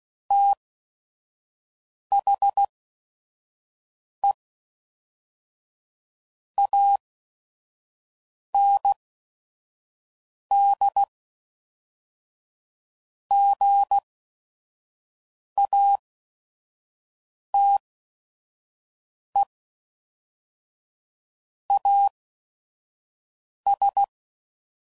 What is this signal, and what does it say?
Morse code, heard as a keyed tone, about 16 wpm, spaced as for 5 wpm, 800 Hz. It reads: THE AND GATE AS
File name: CW Code Groups @5WPM / Technical Subjects / AND Gate